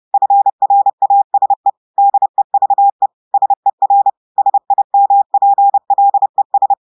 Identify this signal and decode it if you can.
Morse code, heard as a keyed tone, about 30 wpm, 800 Hz. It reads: FRASE DEVE SER SIMPLES
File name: portugais texte en clair 2 vitesse 30 mots minutes